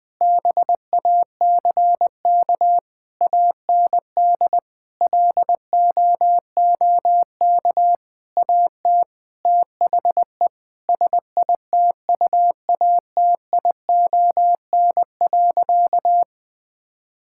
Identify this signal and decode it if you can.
Morse code, heard as a keyed tone, about 20 wpm, 700 Hz. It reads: BACK AND LOOK AT THE SITUATION.
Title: Sentences with 8-Word Phrases - 20wpm